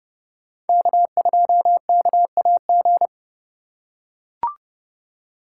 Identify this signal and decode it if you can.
Morse code, heard as a keyed tone, about 27 wpm, 700 Hz. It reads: K2KAG